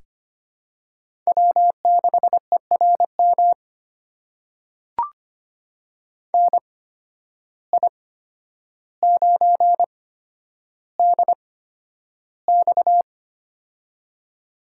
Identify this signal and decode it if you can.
Morse code, heard as a keyed tone, about 25 wpm, spaced as for 11 wpm, 700 Hz. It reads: W6ERM NI9DX
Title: USA Call Signs - 8x Character Spacing - 25wpm